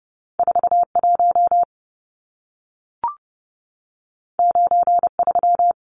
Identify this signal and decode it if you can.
Morse code, heard as a keyed tone, about 30 wpm, 700 Hz. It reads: 41 93
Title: Numbers - 2-Digit 30wpm